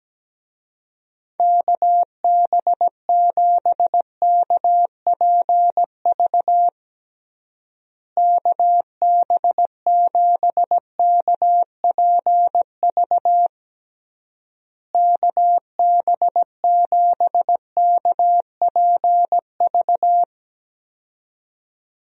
Morse code, heard as a keyed tone, about 17 wpm, 700 Hz. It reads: KB7KPV KB7KPV KB7KPV
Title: USA Call Signs - 6 Characters - Repeated 3x - 17wpm